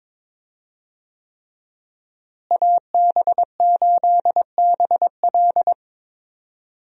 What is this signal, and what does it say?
Morse code, heard as a keyed tone, about 21 wpm, 700 Hz. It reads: AB8BL
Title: USA Call Signs - 5 Characters - Repeated 3x - 22wpm